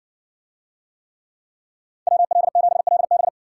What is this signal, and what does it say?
Morse code, heard as a keyed tone, about 58 wpm, 700 Hz. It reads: WL7LB